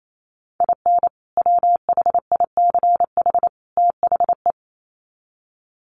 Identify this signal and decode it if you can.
Morse code, heard as a keyed tone, about 28 wpm, 700 Hz. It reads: IN WHICH THE
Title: Sets of 3 Words 28wpm